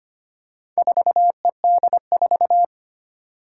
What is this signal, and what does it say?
Morse code, heard as a keyed tone, about 25 wpm, 700 Hz. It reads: 4ED4
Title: Four Challenging Letter-Number Combinations 25wpm